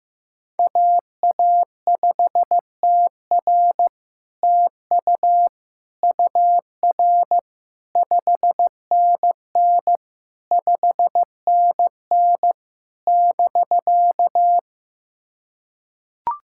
Morse code, heard as a keyed tone, about 15 wpm, 700 Hz. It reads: AA5TR TU UR 5NN 5NN <BK>
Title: SOTA QSO Practice 15wpm